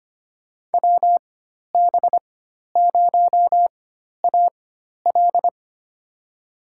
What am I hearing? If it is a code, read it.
Morse code, heard as a keyed tone, about 25 wpm, spaced as for 12 wpm, 700 Hz. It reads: WB0AL
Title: USA Call Signs - 5 Characters - 4x Character Spacing - 25wpm